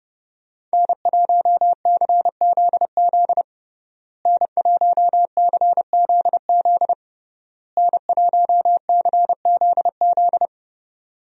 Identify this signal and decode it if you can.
Morse code, heard as a keyed tone, about 30 wpm, 700 Hz. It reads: N1CZZ N1CZZ N1CZZ
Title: USA Call Signs - 5 Characters - Repeated 3x - 30wpm